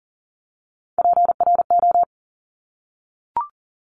Morse code, heard as a keyed tone, about 40 wpm, 700 Hz. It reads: PRO
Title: Top Three Letter Combinations in Words 40wpm